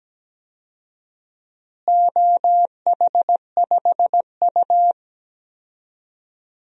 Morse code, heard as a keyed tone, about 17 wpm, 700 Hz. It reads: OH5U
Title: DXCC-100 Call Signs - 4 Characters - Repeated 3x - 17wpm